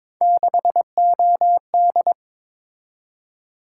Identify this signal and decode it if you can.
Morse code, heard as a keyed tone, about 22 wpm, 700 Hz. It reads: 6OD
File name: practice / 22wpm / Three Letter-Number Combinations - 22wpm